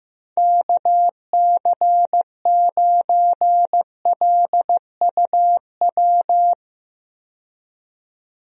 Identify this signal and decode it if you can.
Morse code, heard as a keyed tone, about 15 wpm, 700 Hz. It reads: KC9LUW
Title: USA Call Signs - 6 Characters - 15wpm